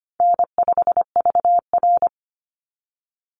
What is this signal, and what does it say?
Morse code, heard as a keyed tone, about 25 wpm, 700 Hz. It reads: N5VR
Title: Easy Call Signs - 25wpm